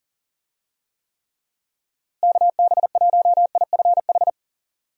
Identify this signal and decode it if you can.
Morse code, heard as a keyed tone, about 40 wpm, 700 Hz. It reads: KB1IFH
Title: USA Call Signs - 6 Characters - Repeated 3x - 40wpm